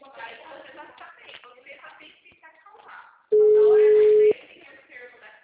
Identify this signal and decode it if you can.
Ringback tone (Europe, 425 Hz), with speech.